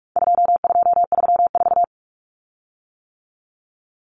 Morse code, heard as a keyed tone, about 44 wpm, 700 Hz. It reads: 2234
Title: Common Cut Numbers - 4-Digit 45wpm